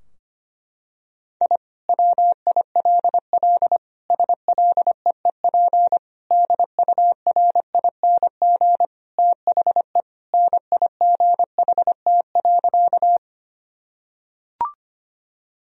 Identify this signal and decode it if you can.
Morse code, heard as a keyed tone, about 25 wpm, 700 Hz. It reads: I WILL SLEEP DURING THE NIGHT.
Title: Sentences from Top 1000 Words - Review Part 2 - 25wpm